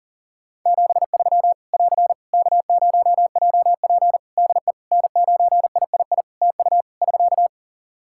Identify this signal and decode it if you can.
Morse code, heard as a keyed tone, about 40 wpm, 700 Hz. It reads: 73 <AR> K0JP DE N9III TU <SK>
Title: Practice QSO - Exchange 4 - Ending - 40wpm